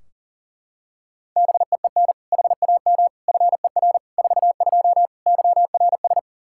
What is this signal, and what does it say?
Morse code, heard as a keyed tone, about 40 wpm, 700 Hz. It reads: BEEN HAM FER 42 YRS
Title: Practice QSO - Exchange 3 - More Info - 40wpm